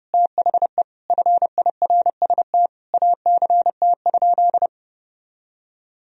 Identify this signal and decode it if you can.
Morse code, heard as a keyed tone, about 30 wpm, 700 Hz. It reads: THE FIRST ACT?
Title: Sentences from Top 1000 Words - Review Part 2 - 30wpm